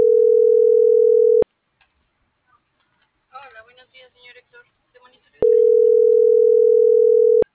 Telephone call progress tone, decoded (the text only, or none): ringback tone